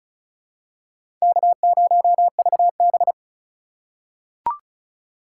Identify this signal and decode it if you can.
Morse code, heard as a keyed tone, about 36 wpm, 700 Hz. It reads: K0VB
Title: USA Call Signs - 4 Characters - 8x Character Spacing - 35wpm